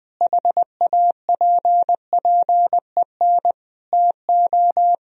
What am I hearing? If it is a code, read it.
Morse code, heard as a keyed tone, about 20 wpm, 700 Hz. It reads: HAPPEN TO
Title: Sentences from Top 500 Words and Abbreviations 20wpm